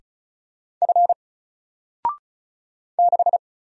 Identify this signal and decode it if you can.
Morse code, heard as a keyed tone, about 36 wpm, 700 Hz. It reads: F 6
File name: Single-Letter-Number-35wpm